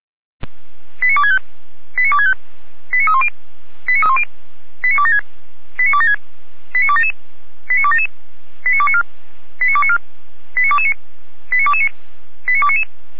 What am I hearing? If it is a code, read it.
Five-tone selective call (ZVEI), heard as quick runs of five tones, tones of about 70 ms. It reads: 89265 89265 89310 89276 89276 89277 89277 89294 89294 89229 89229 89299
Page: SDR/ZVEI